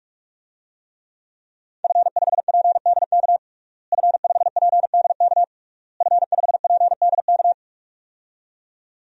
Morse code, heard as a keyed tone, about 41 wpm, 700 Hz. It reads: F5PDK F5PDK F5PDK
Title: DXCC-100 Call Signs - 5 Characters - Repeated 3x - 45wpm